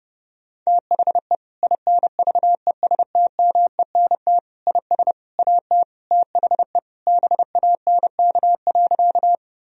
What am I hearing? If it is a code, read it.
Morse code, heard as a keyed tone, about 30 wpm, 700 Hz. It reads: THE INVESTMENT IS AT THE BANK.
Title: Sentences from Top 1000 Words - Review Part 4 - 30wpm